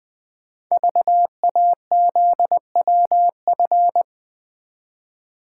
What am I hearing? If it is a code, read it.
Morse code, heard as a keyed tone, about 20 wpm, 700 Hz. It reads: VAZWF
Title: Code Groups - 5 Letter:Numbers - 20wpm